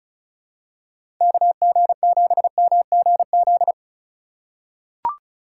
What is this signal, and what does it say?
Morse code, heard as a keyed tone, about 36 wpm, 700 Hz. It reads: KG7MGZ
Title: USA Call Signs - 6 Characters - 2x Character Spacing - 35wpm